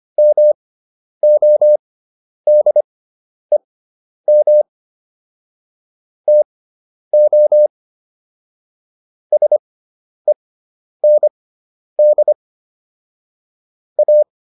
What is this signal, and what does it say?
Morse code, heard as a keyed tone, about 25 wpm, spaced as for 10 wpm, 600 Hz. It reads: MODEM TO SEND A